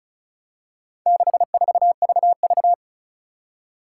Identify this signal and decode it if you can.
Morse code, heard as a keyed tone, about 36 wpm, 700 Hz. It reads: 64VV
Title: Four Challenging Letter-Number Combinations 35wpm